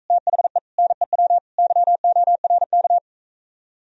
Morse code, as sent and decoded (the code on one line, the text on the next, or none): - .... / -. .-- / -.-- --- .-. -.-
THE NEW YORK